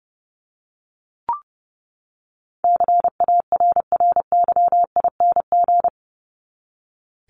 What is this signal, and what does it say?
Morse code, heard as a keyed tone, about 30 wpm, 700 Hz. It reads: CARRYING
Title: ING Suffix - Top 100 Words with Base Words 30wpm